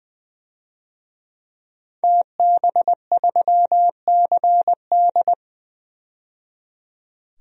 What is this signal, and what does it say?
Morse code, heard as a keyed tone, about 21 wpm, 700 Hz. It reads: TB3CD
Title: DXCC-100 Call Signs - 5 Characters - 20wpm